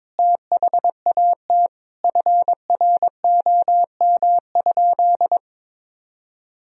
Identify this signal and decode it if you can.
Morse code, heard as a keyed tone, about 22 wpm, 700 Hz. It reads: THAT FROM?
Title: Sentences from Top 1000 Words - Review Part 4 - 22wpm